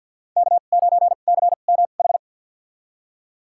Morse code, heard as a keyed tone, about 46 wpm, spaced as for 35 wpm, 700 Hz. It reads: K9CMH
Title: USA Call Signs - 2x Character Spacing - 50wpm